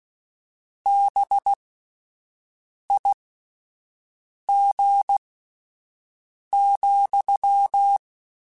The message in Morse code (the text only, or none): BIG,